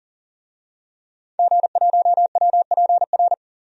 Morse code, heard as a keyed tone, about 40 wpm, 700 Hz. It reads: G1WPR
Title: DXCC-100 Call Signs - 5 Characters - 40wpm